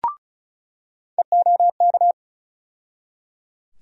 Morse code, heard as a keyed tone, about 36 wpm, 700 Hz. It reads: EOK
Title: USA Call Sign Postfixes - 3 Letters 35wpm